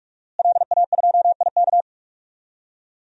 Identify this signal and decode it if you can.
Morse code, heard as a keyed tone, about 46 wpm, 700 Hz. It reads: LA2IK
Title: DXCC-100 Call Signs - 5 Characters - Repeated 2x - 45wpm